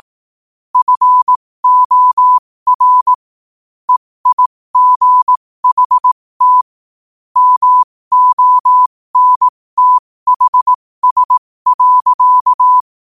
Morse code, heard as a keyed tone, about 18 wpm, spaced as for 15 wpm, 1000 Hz. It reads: FOR EIGHT MONTHS.
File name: qotdfiles / 15wpm